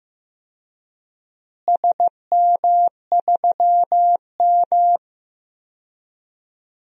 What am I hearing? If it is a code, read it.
Morse code, heard as a keyed tone, about 15 wpm, 700 Hz. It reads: SM3M
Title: DXCC-100 Call Signs - 4 Characters - Repeated 3x - 15wpm